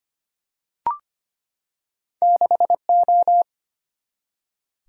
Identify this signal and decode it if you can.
Morse code, heard as a keyed tone, about 25 wpm, 700 Hz. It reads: 6 O
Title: Single-Letter Single-Number 25wpm